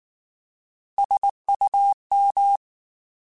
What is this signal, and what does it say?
Morse code, heard as a keyed tone, about 19 wpm, 800 Hz. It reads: SUM
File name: Alphabetical / Alphabetical mp3s / Ss @18